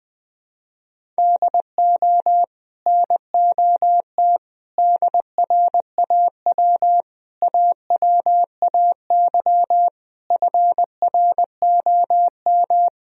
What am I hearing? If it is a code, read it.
Morse code, heard as a keyed tone, about 21 wpm, 700 Hz. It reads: DO NOT DRAW AWAY FROM
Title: Sentences from Top 1000 Words - Review Part 1 - 20wpm